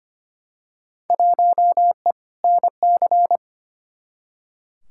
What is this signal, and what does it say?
Morse code, heard as a keyed tone, about 25 wpm, 700 Hz. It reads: 1E NC
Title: Field Day Exchanges - 25wpm